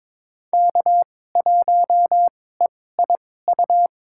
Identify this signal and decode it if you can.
Morse code, heard as a keyed tone, about 22 wpm, spaced as for 16 wpm, 700 Hz. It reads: K1EIU
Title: USA Call Signs - 5 Characters - 2x Character Spacing - 22wpm